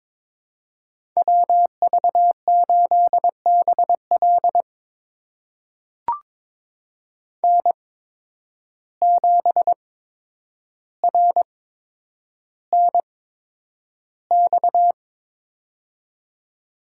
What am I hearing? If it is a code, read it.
Morse code, heard as a keyed tone, about 22 wpm, spaced as for 9 wpm, 700 Hz. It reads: WV8BL N7RNX
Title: USA Call Signs - 5 Characters - 8x Character Spacing - 22wpm